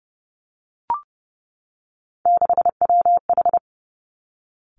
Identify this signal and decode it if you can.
Morse code, heard as a keyed tone, about 30 wpm, 700 Hz. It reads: 6WH
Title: Three Challenging Letter-Number Combinations 30wpm